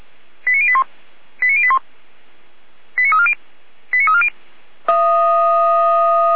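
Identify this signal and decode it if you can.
Five-tone selective call (ZVEI), heard as quick runs of five tones, tones of about 70 ms. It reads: 89081 89081 89340 89340
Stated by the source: ZVEI five-tone selective call 89313, received by SDR.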